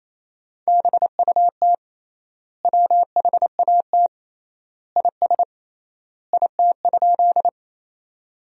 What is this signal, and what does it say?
Morse code, heard as a keyed tone, about 28 wpm, 700 Hz. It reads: BUT WHAT IS IT?
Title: 3x Word-Spacing - Sentences from Top 100 Words - 28wpm